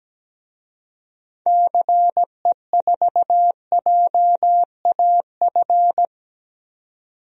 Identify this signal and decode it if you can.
Morse code, heard as a keyed tone, about 17 wpm, 700 Hz. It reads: CE4JAF
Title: DXCC-100 Call Signs - 6 Characters - 17wpm